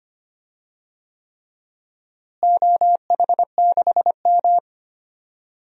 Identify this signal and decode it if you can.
Morse code, heard as a keyed tone, about 25 wpm, 700 Hz. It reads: OH6M
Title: DXCC-100 Call Signs - 4 Characters - Repeated 3x - 25wpm